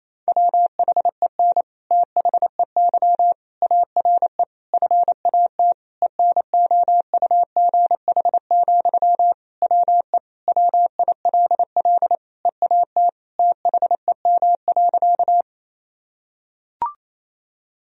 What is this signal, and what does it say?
Morse code, heard as a keyed tone, about 28 wpm, 700 Hz. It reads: WHEN THEY ARE FAT ENOUGH, WE WILL EAT THEM.
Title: Sentences from Top 1000 Words - Review Part 4 - 28wpm